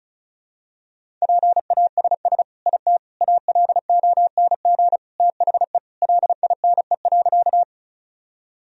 Morse code, heard as a keyed tone, about 36 wpm, 700 Hz. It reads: PASS IT ALONG THE LINE.
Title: Sentences from Top 1000 Words - Review Part 1 - 35wpm